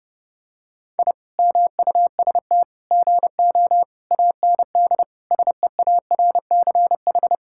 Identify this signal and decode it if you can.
Morse code, heard as a keyed tone, about 30 wpm, 700 Hz. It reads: I MUST GO AND SEARCH
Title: Sentences from Top 1000 Words - Review Part 5 - 30wpm